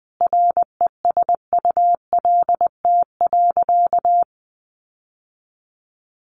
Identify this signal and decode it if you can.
Morse code, heard as a keyed tone, about 20 wpm, 700 Hz. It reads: RESULT.
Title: Sentences from Top 700 Words 20wpm